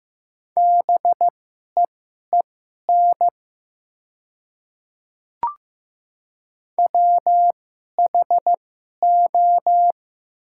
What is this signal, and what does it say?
Morse code, heard as a keyed tone, about 15 wpm, spaced as for 11 wpm, 700 Hz. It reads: BEEN WHO
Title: 2x Character Spacing - Top 500 Words - 15wpm